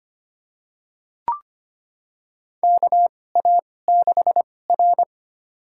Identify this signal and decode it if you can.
Morse code, heard as a keyed tone, about 25 wpm, spaced as for 18 wpm, 700 Hz. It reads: KA6R
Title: USA Call Signs - 4 Characters - 2x Character Spacing - 25wpm